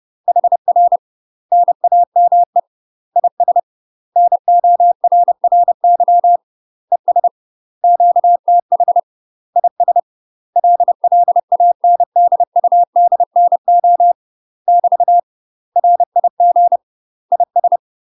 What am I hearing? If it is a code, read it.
Morse code, heard as a keyed tone, about 30 wpm, 700 Hz. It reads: HR NAME IS NORRY ES QTH IS LLANDUDNO <BT> RIG IS
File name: UK MASTER QSOs TRAINING / 30WPM / UK MASTER QSOs 0047